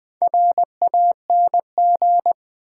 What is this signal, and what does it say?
Morse code, heard as a keyed tone, about 20 wpm, 700 Hz. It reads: RANG
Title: Build A Word - 1 Letter At A Time - 3 to 5-Letter Words - 20wpm